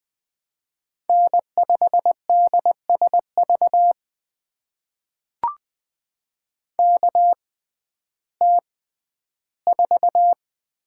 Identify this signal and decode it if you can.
Morse code, heard as a keyed tone, about 20 wpm, spaced as for 12 wpm, 700 Hz. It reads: N5DSV KT4
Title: USA Call Signs - 5 Characters - 6x Character Spacing - 20wpm